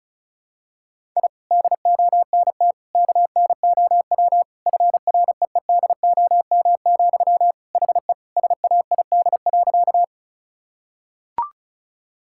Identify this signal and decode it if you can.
Morse code, heard as a keyed tone, about 36 wpm, 700 Hz. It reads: I DONT KNOW FREEDOM, HE SAID.
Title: Sentences from Top 1000 Words - Review Part 5 - 35wpm